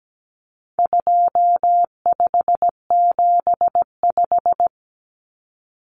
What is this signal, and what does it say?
Morse code, heard as a keyed tone, about 17 wpm, 700 Hz. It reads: 2575
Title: Numbers - 4-Digit 17wpm